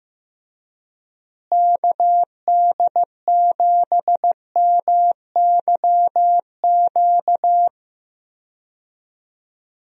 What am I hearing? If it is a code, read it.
Morse code, heard as a keyed tone, about 15 wpm, 700 Hz. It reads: KD7MYQ
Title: USA Call Signs - 6 Characters - Repeated 3x - 15wpm